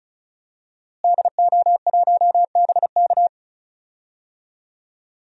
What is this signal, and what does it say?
Morse code, heard as a keyed tone, about 36 wpm, 700 Hz. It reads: DO1BK
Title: DXCC-100 Call Signs - 5 Characters - Repeated 3x - 35wpm